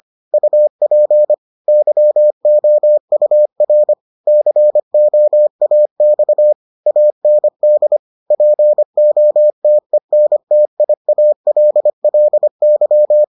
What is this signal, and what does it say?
Morse code, heard as a keyed tone, about 25 wpm, 600 Hz. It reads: UP YOUR COAX AND POTENTIALLY